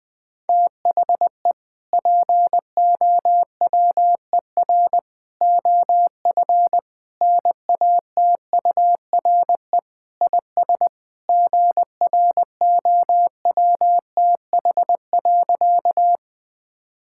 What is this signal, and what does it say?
Morse code, heard as a keyed tone, about 20 wpm, 700 Hz. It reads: THE POWER OF NATURE IS GROWTH.
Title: Sentences from Top 1000 Words - Review Part 5 - 20wpm